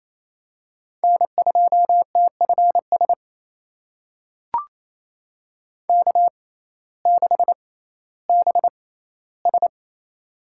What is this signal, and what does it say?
Morse code, heard as a keyed tone, about 28 wpm, spaced as for 15 wpm, 700 Hz. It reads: N2TFS K6BS
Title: USA Call Signs - 6x Character Spacing - 28wpm